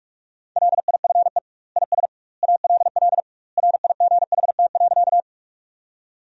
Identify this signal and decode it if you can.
Morse code, heard as a keyed tone, about 46 wpm, 700 Hz. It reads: LIFE IS ALL RIGHT.
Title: 2x Word-Spacing - Sentences from Top 300 Words - 45wpm